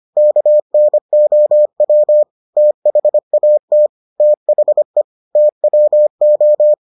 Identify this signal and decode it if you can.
Morse code, heard as a keyed tone, about 25 wpm, 600 Hz. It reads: KNOW THAT THE TWO